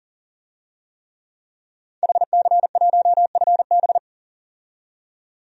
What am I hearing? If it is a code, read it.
Morse code, heard as a keyed tone, about 40 wpm, 700 Hz. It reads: HC1FB
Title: DXCC-100 Call Signs - 5 Characters - 40wpm